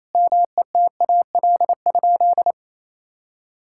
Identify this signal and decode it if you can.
Morse code, heard as a keyed tone, about 28 wpm, 700 Hz. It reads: METAL?
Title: Sentences from Top 1600 Words 28wpm